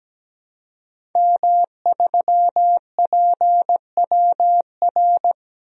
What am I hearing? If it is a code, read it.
Morse code, heard as a keyed tone, about 17 wpm, 700 Hz. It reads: M3PWR
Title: DXCC-100 Call Signs - 5 Characters - Repeated 2x - 17wpm